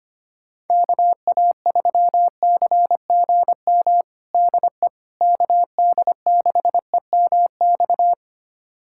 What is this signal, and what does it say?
Morse code, heard as a keyed tone, about 25 wpm, 700 Hz. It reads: KA3CGM DE KD6EMX
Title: USA Callsign DE Callsign 25wpm